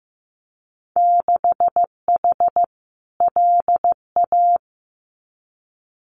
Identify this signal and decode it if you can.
Morse code, heard as a keyed tone, about 15 wpm, 700 Hz. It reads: 6H LA
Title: Winter Field Day Exchanges Encore 15wpm